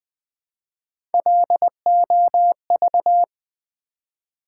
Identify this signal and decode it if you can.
Morse code, heard as a keyed tone, about 20 wpm, 700 Hz. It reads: LOV
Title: Build A Word - 1 Letter At A Time - 4-Letter Words - 20wpm